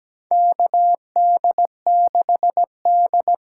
Morse code, heard as a keyed tone, about 17 wpm, 700 Hz. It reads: KD6D
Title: USA Call Signs - 6 Characters - 8x Character Spacing - 17wpm